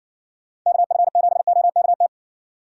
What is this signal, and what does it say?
Morse code, heard as a keyed tone, about 62 wpm, 700 Hz. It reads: KF7OKT
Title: Speed Racing - USA Call Signs 50wpm